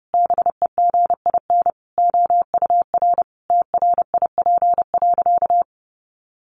Morse code, heard as a keyed tone, about 30 wpm, 700 Hz. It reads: BEGIN OUR TRIP.
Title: Sentences from Top 1300 Words 30wpm